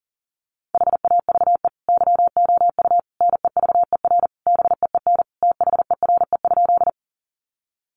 Morse code, heard as a keyed tone, about 40 wpm, 700 Hz. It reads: HAVE YOU NEVER BEEN THERE?
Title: Sentences from Top 300 Words 40wpm